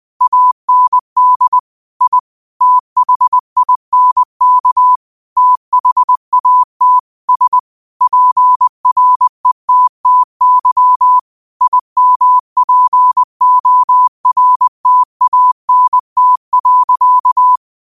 Morse code, heard as a keyed tone, about 20 wpm, 1000 Hz. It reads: AND I THINK THAT S PRETTY IMPORTANT.
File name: qotdfiles / 20wpm